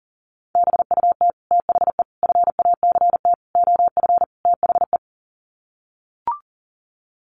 Morse code, heard as a keyed tone, about 40 wpm, 700 Hz. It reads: BUT THE FACT OF THE
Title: Sets of 5 Words Encore2 40wpm